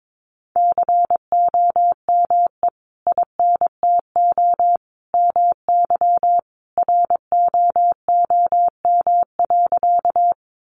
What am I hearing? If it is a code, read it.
Morse code, heard as a keyed tone, about 22 wpm, 700 Hz. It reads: COME INTO MY ROOM.
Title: Sentences from Top 600 Words 22wpm